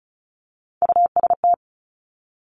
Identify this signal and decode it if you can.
Morse code, heard as a keyed tone, about 36 wpm, 700 Hz. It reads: UST
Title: Top Three Letter Combinations in Words 35wpm